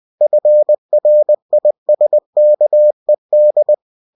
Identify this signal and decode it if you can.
Morse code, heard as a keyed tone, about 20 wpm, 600 Hz. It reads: FRISKED
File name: cw-books / winnie-the-pooh / winnie-20wpm-page-0032